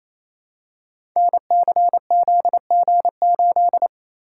Morse code, heard as a keyed tone, about 27 wpm, 700 Hz. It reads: NCZG8
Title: Code Groups - 5 Letter:Numbers - 28wpm